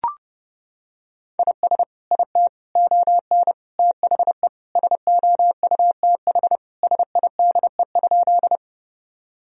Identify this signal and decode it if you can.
Morse code, heard as a keyed tone, about 30 wpm, 700 Hz. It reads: IS IT ON THE SOUTH SIDE?
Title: Sentences from Top 1000 Words - Review Part 2 - 30wpm